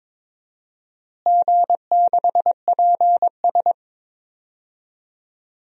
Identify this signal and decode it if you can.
Morse code, heard as a keyed tone, about 21 wpm, 700 Hz. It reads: G6PS
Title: DXCC-100 Call Signs - 4 Characters - Repeated 2x - 22wpm